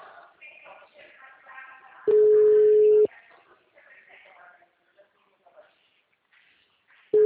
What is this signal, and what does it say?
Ringback tone (Europe, 425 Hz), with speech.